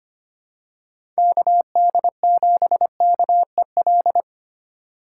Morse code, heard as a keyed tone, about 25 wpm, 700 Hz. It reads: KD7KEL
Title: USA Call Signs - 3x Character Spacing - 25wpm